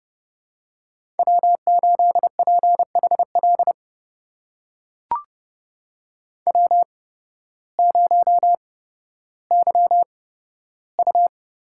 Morse code, heard as a keyed tone, about 29 wpm, spaced as for 14 wpm, 700 Hz. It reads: W8PHL W0YU
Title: USA Call Signs - 8x Character Spacing - 30wpm